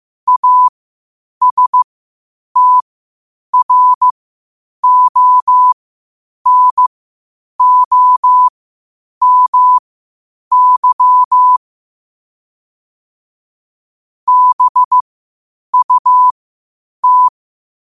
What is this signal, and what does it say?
Morse code, heard as a keyed tone, about 15 wpm, spaced as for 8 wpm, 1000 Hz. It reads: ASTRONOMY BUT
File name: qotdfiles / 8wpm